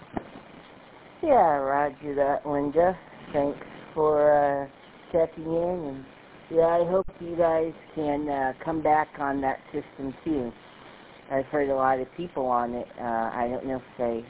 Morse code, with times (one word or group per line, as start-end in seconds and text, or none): none